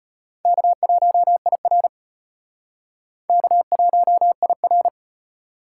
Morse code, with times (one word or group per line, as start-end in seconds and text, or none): 0.5-1.9 K1IR
3.3-4.9 K1IR